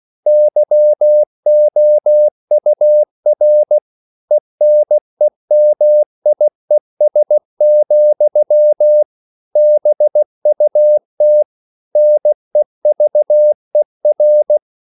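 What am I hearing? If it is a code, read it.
Morse code, heard as a keyed tone, about 16 wpm, 600 Hz. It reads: YOUR ENEMIES, BUT NEVER
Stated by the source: English quotations sent in CW as practice at 16 words per minute.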